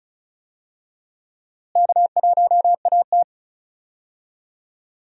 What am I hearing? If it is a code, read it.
Morse code, heard as a keyed tone, about 34 wpm, 700 Hz. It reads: K1AT
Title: USA Call Signs - 4 Characters - Repeated 3x - 35wpm